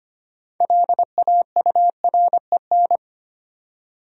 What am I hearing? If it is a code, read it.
Morse code, heard as a keyed tone, about 25 wpm, 700 Hz. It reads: LAUREN
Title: USA Common Names - 25wpm